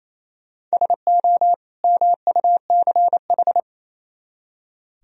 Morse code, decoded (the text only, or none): SO MUCH